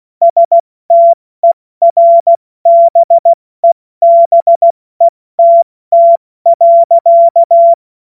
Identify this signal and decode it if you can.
Morse code, heard as a keyed tone, about 16 wpm, 680 Hz. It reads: STERBEBETT.
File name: Morsekurs / Witze / Witz 06. September 2022 70 F80